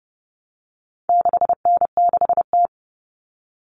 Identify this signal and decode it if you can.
Morse code, heard as a keyed tone, about 29 wpm, 700 Hz. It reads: 6N6T
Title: All Cut Numbers - 4-Digit 30wpm